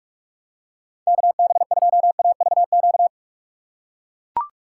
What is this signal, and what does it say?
Morse code, heard as a keyed tone, about 42 wpm, 700 Hz. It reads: KB2AVQ
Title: USA Call Signs - 6 Characters - 3x Character Spacing - 45wpm